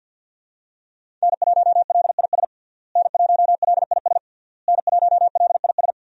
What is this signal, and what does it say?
Morse code, heard as a keyed tone, about 52 wpm, 700 Hz. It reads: N1LIS N1LIS N1LIS
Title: USA Call Signs - 5 Characters - Repeated 3x - 50wpm